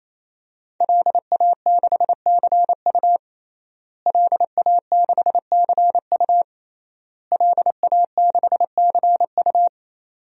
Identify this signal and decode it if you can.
Morse code, heard as a keyed tone, about 27 wpm, 700 Hz. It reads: LA6CU LA6CU LA6CU
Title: DXCC-100 Call Signs - 5 Characters - Repeated 3x - 28wpm